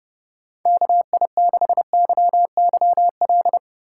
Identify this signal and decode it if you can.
Morse code, heard as a keyed tone, about 30 wpm, 700 Hz. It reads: KI6YYL